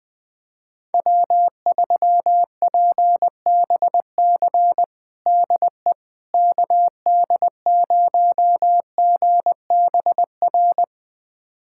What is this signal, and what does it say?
Morse code, heard as a keyed tone, about 21 wpm, 700 Hz. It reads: W3PBC DE KD0GBR
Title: USA Callsign DE Callsign 20wpm